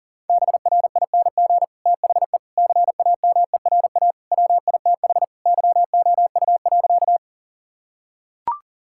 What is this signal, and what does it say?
Morse code, heard as a keyed tone, about 40 wpm, 700 Hz. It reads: BRING THE CAMERA WITH YOU.
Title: Sentences from Top 1000 Words - Review Part 5 - 40wpm